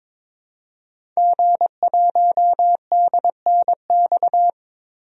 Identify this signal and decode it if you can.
Morse code, heard as a keyed tone, about 22 wpm, 700 Hz. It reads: G1DNX
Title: DXCC-100 Call Signs - 5 Characters - Repeated 3x - 22wpm